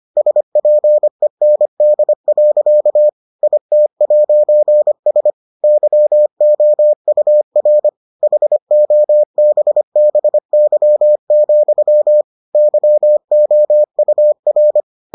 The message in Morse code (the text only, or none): SPEND. IT'S YOUR HOBBY, YOUR